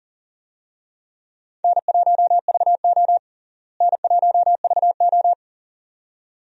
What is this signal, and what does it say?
Morse code, heard as a keyed tone, about 40 wpm, 700 Hz. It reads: N1VO N1VO
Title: USA Call Signs - 4 Characters - Repeated 2x - 40wpm